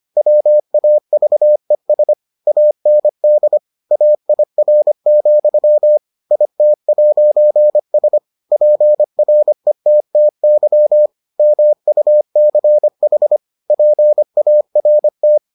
Morse code, heard as a keyed tone, about 25 wpm, 600 Hz. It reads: WAVES AND AIR, IT'S PRETTY MUCH PART